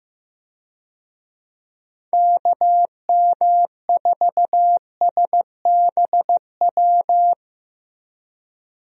Morse code, heard as a keyed tone, about 15 wpm, 700 Hz. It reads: KM4SBW
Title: USA Call Signs - 6 Characters - Repeated 3x - 15wpm